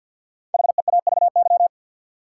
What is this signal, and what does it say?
Morse code, heard as a keyed tone, about 47 wpm, 700 Hz. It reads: HEAVY
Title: Sentences from Top 1000 Words - Review Part 6 - New Word Drill - 50wpm